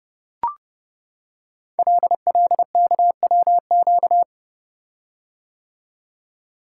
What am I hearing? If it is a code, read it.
Morse code, heard as a keyed tone, about 30 wpm, 700 Hz. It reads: LLKWQ
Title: Code Groups - 5 Letter:Numbers - 30wpm